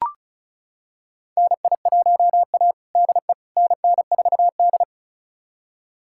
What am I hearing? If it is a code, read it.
Morse code, heard as a keyed tone, about 36 wpm, 700 Hz. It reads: NI1A DE NN4D
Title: USA Callsign DE Callsign - 4 Characters - 35wpm